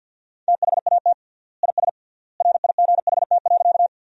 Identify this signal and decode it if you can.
Morse code, heard as a keyed tone, about 53 wpm, 700 Hz. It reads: THAT IS RIGHT.
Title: 3x Word-Spacing - Sentences from Top 200 Words - 50wpm